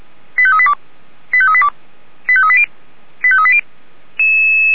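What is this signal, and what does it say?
Five-tone selective call (ZVEI), heard as quick runs of five tones, tones of about 70 ms. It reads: 85382 85382 85380 85380
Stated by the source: ZVEI five-tone selective call received on SDR.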